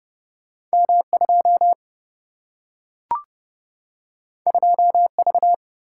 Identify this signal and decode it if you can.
Morse code, heard as a keyed tone, about 30 wpm, 700 Hz. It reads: M 2 2 V
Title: Single-Letter Single-Number 30wpm